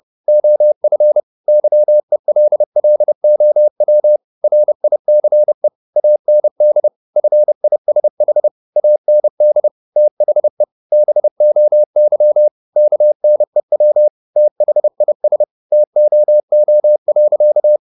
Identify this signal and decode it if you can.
Morse code, heard as a keyed tone, about 30 wpm, 600 Hz. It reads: OF YELLOW RICE AND FISH AND THE BOY KNEW THIS TOO.
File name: the old man and the sea / the old man and the sea 1 30wpm